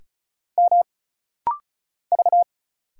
Morse code, heard as a keyed tone, about 34 wpm, 700 Hz. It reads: M V
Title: Single Letters - Rapid-Fire 35wpm